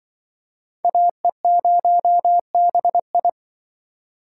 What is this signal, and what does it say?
Morse code, heard as a keyed tone, about 25 wpm, 700 Hz. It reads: AE0BI
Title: Speed Racing - USA 2x2 Call Signs 20wpm